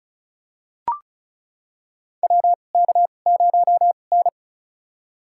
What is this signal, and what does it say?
Morse code, heard as a keyed tone, about 36 wpm, spaced as for 25 wpm, 700 Hz. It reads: WK0N
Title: USA Call Signs - 2x Character Spacing - 35wpm